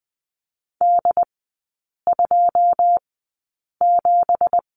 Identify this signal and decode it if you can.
Morse code, heard as a keyed tone, about 20 wpm, spaced as for 8 wpm, 700 Hz. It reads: D27
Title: 172 Lesson 28e - Number 9 - Three Characters 20wpm